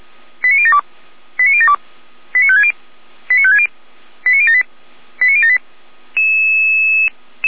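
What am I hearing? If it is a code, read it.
Five-tone selective call (ZVEI), heard as quick runs of five tones, tones of about 70 ms. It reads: 89072 89072 89560 89560 89078 89078
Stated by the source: ZVEI five-tone selective call received by SDR.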